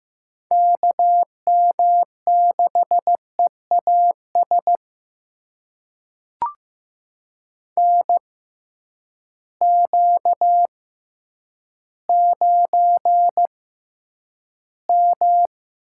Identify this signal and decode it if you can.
Morse code, heard as a keyed tone, about 15 wpm, spaced as for 9 wpm, 700 Hz. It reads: KM6EAS NQ9M
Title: USA Call Signs - 6x Character Spacing - 15wpm